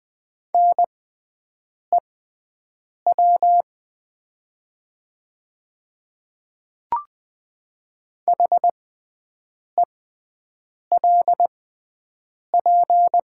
Morse code, heard as a keyed tone, about 20 wpm, spaced as for 7 wpm, 700 Hz. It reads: NEW HELP
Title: 6x Character Spacing - Top 500 Words - 20wpm